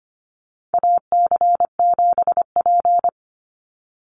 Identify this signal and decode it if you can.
Morse code, heard as a keyed tone, about 25 wpm, 700 Hz. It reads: AC7P
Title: NA SOTA Chaser Callsigns 25wpm